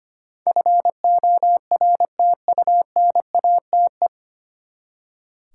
Morse code, heard as a keyed tone, about 25 wpm, 700 Hz. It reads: FORTUNATE
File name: Nine Letter Words - 25wpm